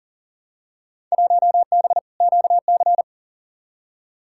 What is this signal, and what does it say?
Morse code, heard as a keyed tone, about 40 wpm, 700 Hz. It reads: 1B QC
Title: Field Day Exchanges - 40wpm